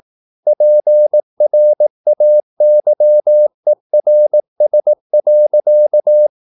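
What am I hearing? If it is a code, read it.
Morse code, heard as a keyed tone, about 18 wpm, 600 Hz. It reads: PRAYERS.
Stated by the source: Morse practice at 18 wpm, English text, from CW med Gnister.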